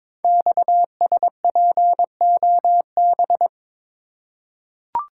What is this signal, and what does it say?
Morse code, heard as a keyed tone, about 22 wpm, 700 Hz. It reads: XSPOB
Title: Code Groups - 5 Letters - 22wpm